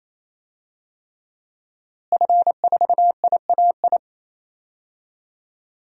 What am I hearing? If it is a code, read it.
Morse code, heard as a keyed tone, about 28 wpm, 700 Hz. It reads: F4IAI
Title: DXCC-100 Call Signs - 5 Characters - Repeated 3x - 28wpm